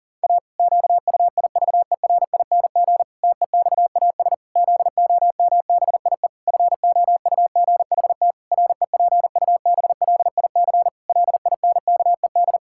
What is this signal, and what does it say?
Morse code, heard as a keyed tone, about 37 wpm, 700 Hz. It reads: A QUIVERING TEXAS ZOMBIE FOUGHT REPUBLIC LINKED
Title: Pangram Challenge v2 - 40wpm